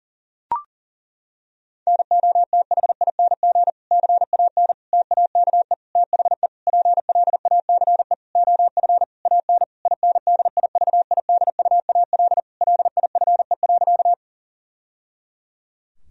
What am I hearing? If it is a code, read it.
Morse code, heard as a keyed tone, about 37 wpm, 700 Hz. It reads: NOTHING CAN TAKE THE PLACE OF AN INDIVIDUAL LIFE.
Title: Sentences from Top 1000 Words - Review Part 4 - 40wpm